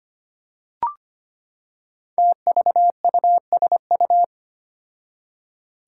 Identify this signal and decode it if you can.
Morse code, heard as a keyed tone, about 25 wpm, 700 Hz. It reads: TVUSU
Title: Five Challenging Letter-Number Combinations 25wpm